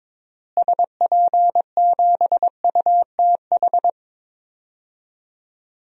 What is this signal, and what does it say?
Morse code, heard as a keyed tone, about 22 wpm, 700 Hz. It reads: SP7UTH
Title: DXCC-100 Call Signs - 6 Characters - Repeated 2x - 22wpm